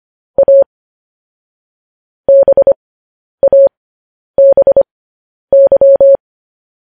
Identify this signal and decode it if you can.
Morse code, heard as a keyed tone, about 25 wpm, spaced as for 10 wpm, 550 Hz. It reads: A BABY
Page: News Headlines in Morse Code at 10 WPM